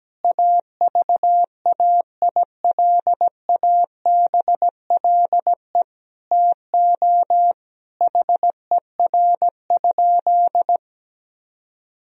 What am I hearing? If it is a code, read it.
Morse code, heard as a keyed tone, about 17 wpm, 700 Hz. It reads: AVAILABLE TO HER?
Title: Sentences from Top 1000 Words - Review Part 3 - 17wpm